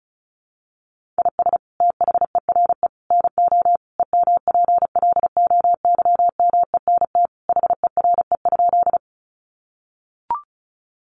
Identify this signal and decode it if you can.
Morse code, heard as a keyed tone, about 34 wpm, 700 Hz. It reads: IS THERE NO EMPLOYMENT HERE?
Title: Sentences from Top 1500 Words 35wpm